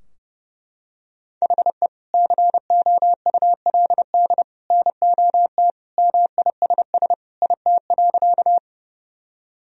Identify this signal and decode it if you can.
Morse code, heard as a keyed tone, about 32 wpm, 700 Hz. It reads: HE COULD NOT MISS IT.
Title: Sentences from Top 1000 Words - Review Part 3 - 30wpm